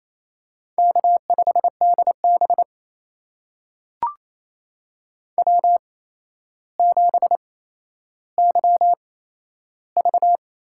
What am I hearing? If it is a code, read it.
Morse code, heard as a keyed tone, about 28 wpm, spaced as for 12 wpm, 700 Hz. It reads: K5DB W7YV
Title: USA Call Signs - 8x Character Spacing - 28wpm